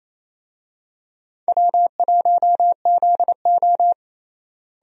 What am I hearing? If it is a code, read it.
Morse code, heard as a keyed tone, about 28 wpm, 700 Hz. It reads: W1ZO
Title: USA Call Signs - 4 Characters - Repeated 2x - 28wpm